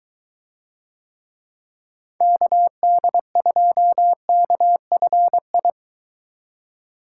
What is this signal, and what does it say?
Morse code, heard as a keyed tone, about 20 wpm, 700 Hz. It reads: KD2KFI